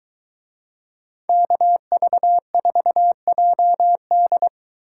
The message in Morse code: KV4JD

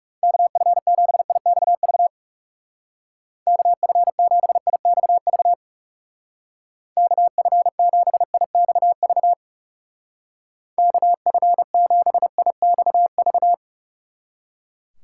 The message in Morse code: KF7IXV KF7IXV KF7IXV KF7IXV

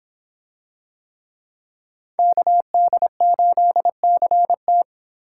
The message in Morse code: KD8CT

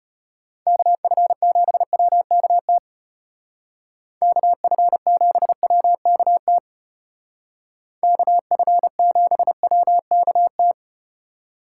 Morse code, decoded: KF7WKT KF7WKT KF7WKT

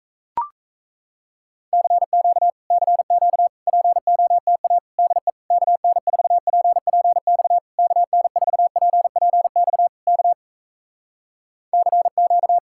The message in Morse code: CQ CQ POTA DE KN4PPX KN4PPX K CQ

